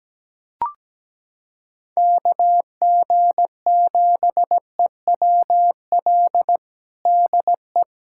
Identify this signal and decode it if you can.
Morse code, heard as a keyed tone, about 17 wpm, 700 Hz. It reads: KG7EWL DE
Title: USA Callsign DE Callsign - 6 Characters - 17wpm